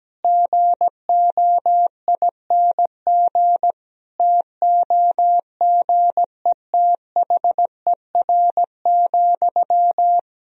Morse code, heard as a keyed tone, about 17 wpm, 700 Hz. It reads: GOING TOGETHER,
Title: Sentences from Top 1000 Words - Review Part 4 - 17wpm